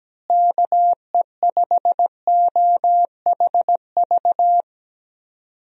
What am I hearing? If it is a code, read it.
Morse code, heard as a keyed tone, about 17 wpm, 700 Hz. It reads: KE5OHV
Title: USA Call Signs - 6 Characters - Repeated 3x - 17wpm